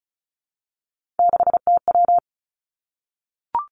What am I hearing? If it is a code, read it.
Morse code, heard as a keyed tone, about 36 wpm, 700 Hz. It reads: 6TW
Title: Three Challenging Letter-Number Combinations 35wpm